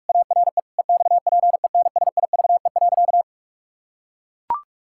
Morse code, heard as a keyed tone, about 44 wpm, 700 Hz. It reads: ARE EXPENSIVE.
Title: Pangram Challenge v2 - 45wpm